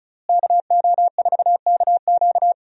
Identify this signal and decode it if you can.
Morse code, heard as a keyed tone, about 36 wpm, 700 Hz. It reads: KO4KQ
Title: USA Call Signs - 5 Characters - Repeated 3x - 35wpm